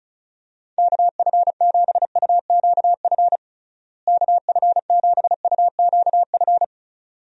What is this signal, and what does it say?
Morse code, heard as a keyed tone, about 36 wpm, 700 Hz. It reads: KF7UQF KF7UQF
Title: USA Call Signs - 6 Characters - Repeated 2x - 35wpm